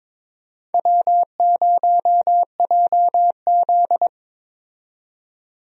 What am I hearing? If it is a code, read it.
Morse code, heard as a keyed tone, about 22 wpm, 700 Hz. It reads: W0JZ
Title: USA Call Signs - 4 Characters - 22wpm